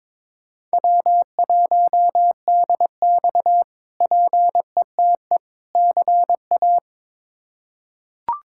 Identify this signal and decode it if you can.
Morse code, heard as a keyed tone, about 22 wpm, 700 Hz. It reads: W1DX PETE CA
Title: K1USN SST QSO Practice - 22wpm